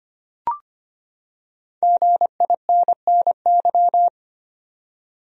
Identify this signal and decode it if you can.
Morse code, heard as a keyed tone, about 25 wpm, 700 Hz. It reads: GINNY
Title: USA Common Names - Max 5 Letters - 25wpm